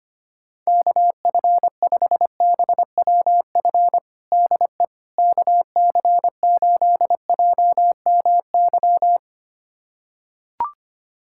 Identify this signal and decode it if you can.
Morse code, heard as a keyed tone, about 25 wpm, 700 Hz. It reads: KF5BWF DE KC8JMY
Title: USA Callsign DE Callsign - 6 Characters - 25wpm